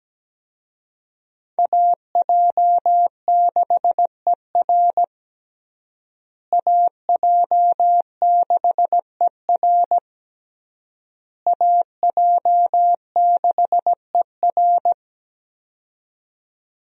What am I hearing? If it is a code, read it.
Morse code, heard as a keyed tone, about 17 wpm, 700 Hz. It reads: AJ6ER AJ6ER AJ6ER
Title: USA Call Signs - 5 Characters - Repeated 3x - 17wpm